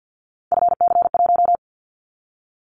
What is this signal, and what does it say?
Morse code, heard as a keyed tone, about 50 wpm, 700 Hz. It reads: FC1
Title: Callsign Prefixes for Top 25 DXCC - 50wpm